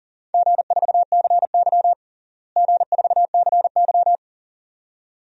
Morse code, heard as a keyed tone, about 40 wpm, 700 Hz. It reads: G4CY G4CY
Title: DXCC-100 Call Signs - 4 Characters - Repeated 2x - 40wpm